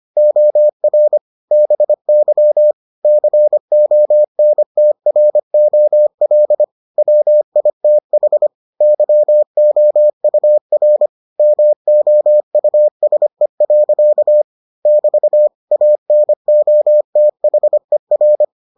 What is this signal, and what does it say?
Morse code, heard as a keyed tone, about 25 wpm, 600 Hz. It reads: OR BY CONTROL WITH YOUR MOUSE. <BT> ANOTHER